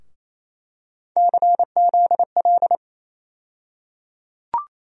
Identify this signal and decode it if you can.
Morse code, heard as a keyed tone, about 28 wpm, 700 Hz. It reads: CZL
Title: USA Call Sign Postfixes - 3 Letters 28wpm